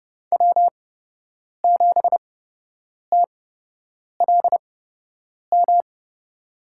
Morse code, heard as a keyed tone, about 30 wpm, spaced as for 8 wpm, 700 Hz. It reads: W7TLM
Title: USA Call Signs - 8x Character Spacing - 30wpm